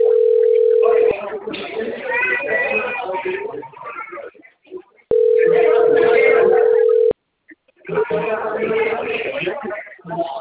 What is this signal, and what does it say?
Ringback tone (North America), 440 + 480 Hz, with speech.
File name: rg-1600-unknown- - - .36108